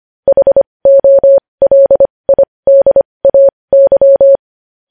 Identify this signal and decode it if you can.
Morse code, heard as a keyed tone, about 25 wpm, spaced as for 20 wpm, 550 Hz. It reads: HOLIDAY
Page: News headlines in Morse code at 20 wpm